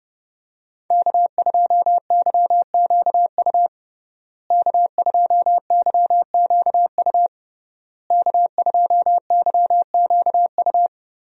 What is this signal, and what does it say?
Morse code, heard as a keyed tone, about 31 wpm, 700 Hz. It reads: K2YQU K2YQU K2YQU
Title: USA Call Signs - 5 Characters - Repeated 3x - 30wpm